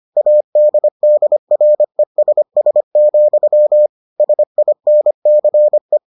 Morse code, heard as a keyed tone, about 25 wpm, 600 Hz. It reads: ADDRESS, SINCE